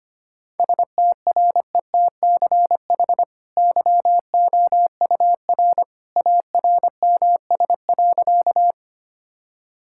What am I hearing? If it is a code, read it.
Morse code, heard as a keyed tone, about 25 wpm, 700 Hz. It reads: STRETCH YOUR ARMS.
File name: Easy Sentences with Hints - 25wpm